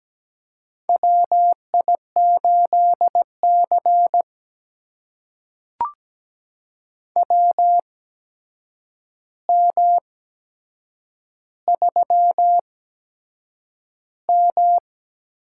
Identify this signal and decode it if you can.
Morse code, heard as a keyed tone, about 17 wpm, spaced as for 7 wpm, 700 Hz. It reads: WI8C WM3M
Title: USA Call Signs - 4 Characters - 8x Character Spacing - 17wpm